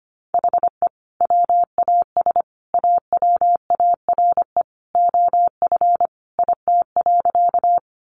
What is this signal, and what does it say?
Morse code, heard as a keyed tone, about 25 wpm, 700 Hz. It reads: HE WAS AWARE OF IT.
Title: Sentences from Top 1200 Words 25wpm